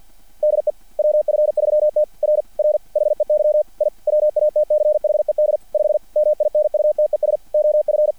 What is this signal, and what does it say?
Morse code, heard as a keyed tone, about 50 wpm, 600 Hz. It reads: DE WF3T R R HEY I WATCHED 5 MINUTES OF
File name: archived / CRSnet